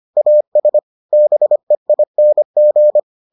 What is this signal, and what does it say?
Morse code, heard as a keyed tone, about 25 wpm, 600 Hz. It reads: AS BEING